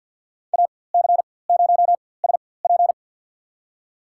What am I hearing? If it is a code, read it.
Morse code, heard as a keyed tone, about 53 wpm, spaced as for 24 wpm, 700 Hz. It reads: AC0SP
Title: USA Call Signs - 5 Characters - 4x Character Spacing - 50wpm